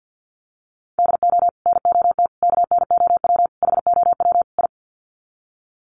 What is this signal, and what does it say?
Morse code, heard as a keyed tone, about 50 wpm, 700 Hz. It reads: DO NOT KNOW HOW I
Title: Sets of 5 Words Encore2 50wpm